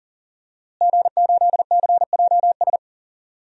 Morse code, heard as a keyed tone, about 37 wpm, 700 Hz. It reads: G8CJS
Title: DXCC-100 Call Signs - 5 Characters - Repeated 3x - 40wpm